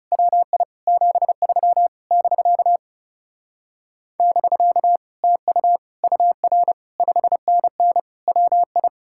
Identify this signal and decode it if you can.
Morse code, heard as a keyed tone, about 40 wpm, 700 Hz. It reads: WI 73 <BK> <BK> TU UR 5NN WI